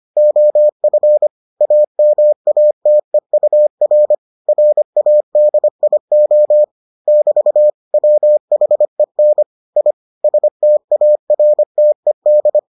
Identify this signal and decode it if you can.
Morse code, heard as a keyed tone, about 25 wpm, 600 Hz. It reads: OF AMATEUR RADIO <BT> WHEN I STARTED